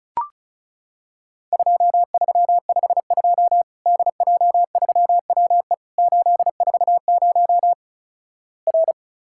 Morse code, heard as a keyed tone, about 36 wpm, 700 Hz. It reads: 2352 DJ3WE 840 R
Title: Worked All Europe DX Contest - QTC Exchange - 35wpm